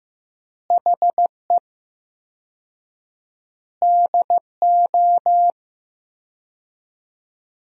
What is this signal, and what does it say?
Morse code, heard as a keyed tone, about 15 wpm, 700 Hz. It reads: HE DO
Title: 4x Word-Spacing - Sentences from Top 100 Words - 15wpm